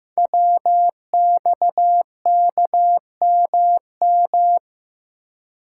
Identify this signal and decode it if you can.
Morse code, heard as a keyed tone, about 15 wpm, 700 Hz. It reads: WXKMM
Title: Code Groups - 5 Letters - 15wpm